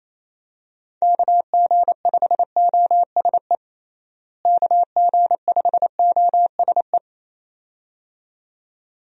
Morse code, heard as a keyed tone, about 28 wpm, 700 Hz. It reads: KG5OSE KG5OSE
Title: USA Call Signs - 6 Characters - Repeated 2x - 28wpm